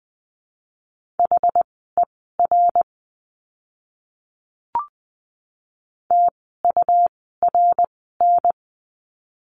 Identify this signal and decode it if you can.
Morse code, heard as a keyed tone, about 20 wpm, spaced as for 14 wpm, 700 Hz. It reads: HER TURN
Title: 206 Lesson 33f - Letter j - Words 20wpm